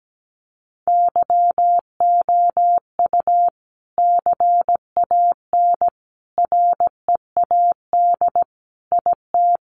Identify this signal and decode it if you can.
Morse code, heard as a keyed tone, about 17 wpm, 700 Hz. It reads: YOU CAN READ IT
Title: Sentences from Top 600 Words 17wpm